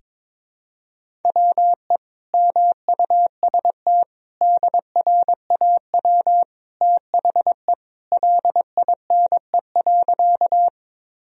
Morse code, heard as a keyed tone, about 22 wpm, 700 Hz. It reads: WE MUST DRAW THE LINE.
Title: Sentences from Top 1000 Words - Review Part 1 - 22wpm